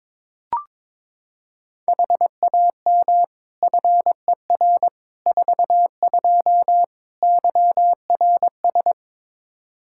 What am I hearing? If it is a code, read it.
Morse code, heard as a keyed tone, about 22 wpm, 700 Hz. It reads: HAM FER 42 YRS
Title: Practice QSO - Exchange 3 - More Info - 22wpm